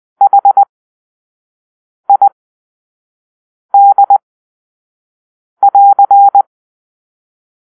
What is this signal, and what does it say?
Morse code, heard as a keyed tone, about 20 wpm, spaced as for 5 wpm, 800 Hz. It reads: HID<AR>
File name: arrl / Grp-2 Drill-4